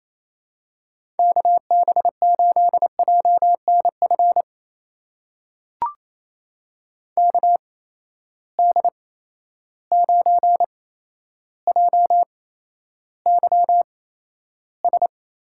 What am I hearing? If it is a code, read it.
Morse code, heard as a keyed tone, about 28 wpm, spaced as for 12 wpm, 700 Hz. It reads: KB8JNF KD9JYS
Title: USA Call Signs - 6 Characters - 8x Character Spacing - 28wpm